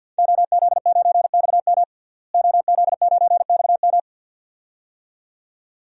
Morse code, heard as a keyed tone, about 49 wpm, 700 Hz. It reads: OZ9XM OZ9XM
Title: DXCC-100 Call Signs - 5 Characters - Repeated 2x - 50wpm